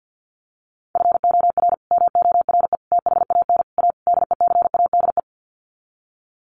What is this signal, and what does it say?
Morse code, heard as a keyed tone, about 50 wpm, 700 Hz. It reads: FOR MORE THAN A DECADE
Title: Sets of 5 Words 50wpm